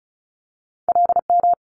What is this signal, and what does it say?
Morse code, heard as a keyed tone, about 36 wpm, 700 Hz. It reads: LM
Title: Multi-letter Phonemes Quick Fire 35wpm